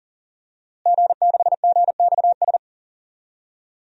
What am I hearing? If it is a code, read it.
Morse code, heard as a keyed tone, about 37 wpm, 700 Hz. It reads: G6GXS
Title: DXCC-100 Call Signs - 5 Characters - Repeated 2x - 40wpm